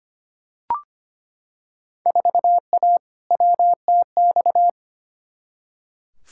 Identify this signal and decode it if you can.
Morse code, heard as a keyed tone, about 25 wpm, 700 Hz. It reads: 4A WTX
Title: Field Day Exchanges - 25wpm